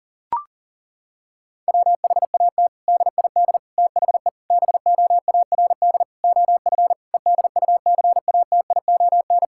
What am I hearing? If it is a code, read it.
Morse code, heard as a keyed tone, about 37 wpm, 700 Hz. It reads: WHAT DID THE BOARD OF EDUCATION
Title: Sentences from Top 1000 Words - Review Part 6 - 40wpm